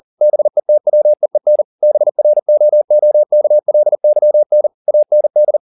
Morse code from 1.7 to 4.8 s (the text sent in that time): BROOKLYN